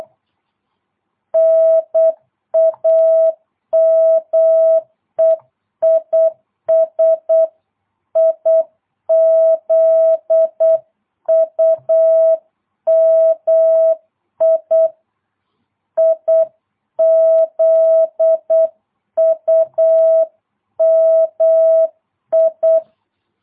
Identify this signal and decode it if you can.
Morse code, heard as a keyed tone, about 8 wpm, 650 Hz. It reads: NAME IS IZUMI IZUMI